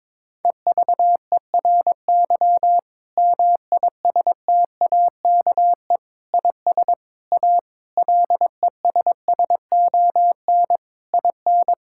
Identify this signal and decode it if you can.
Morse code, heard as a keyed tone, about 22 wpm, 700 Hz. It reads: EVERY MISTAKE IS A LESSON IN
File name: Difficult Sentences with Hints - 22wpm